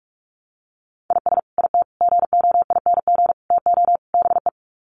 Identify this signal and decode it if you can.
Morse code, heard as a keyed tone, about 44 wpm, 700 Hz. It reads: IS IT GOING TO BE
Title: Sets of 5 Words Encore2 45wpm